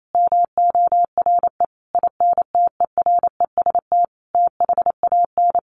Morse code, heard as a keyed tone, about 28 wpm, 700 Hz. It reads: MORE INTEREST THAN
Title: Sentences from Top 400 Words 28wpm